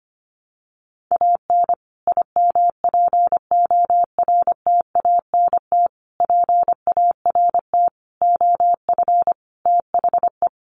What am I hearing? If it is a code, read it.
Morse code, heard as a keyed tone, about 25 wpm, 700 Hz. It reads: AN IMPORTANT PART OF THE
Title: Sets of 5 Words Encore 25wpm